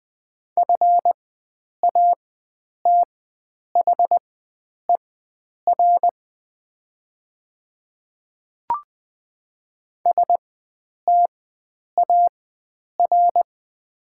Morse code, heard as a keyed tone, about 20 wpm, spaced as for 9 wpm, 700 Hz. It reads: FATHER STAR
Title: 4x Character Spacing - Top 500 Words - 20wpm